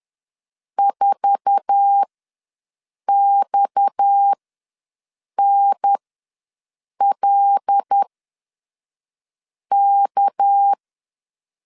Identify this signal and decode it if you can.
Morse code, heard as a keyed tone, about 10 wpm, spaced as for 5 wpm, 800 Hz. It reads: 4XNLK
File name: LEZIONI / 8 YJ45 / 8 YJ4 ws 10wpm 4020